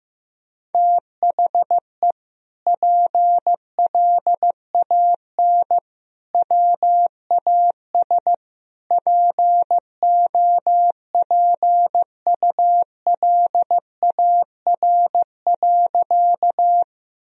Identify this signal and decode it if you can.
Morse code, heard as a keyed tone, about 15 wpm, 700 Hz. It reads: THE PLAN WAS POPULAR.
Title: Sentences from Top 1000 Words - Review Part 3 - 15wpm